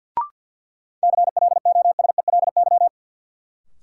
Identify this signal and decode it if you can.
Morse code, heard as a keyed tone, about 48 wpm, 700 Hz. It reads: CLOSELY
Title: Sentences from Top 1400 Words - New Word Drill Alt - 50wpm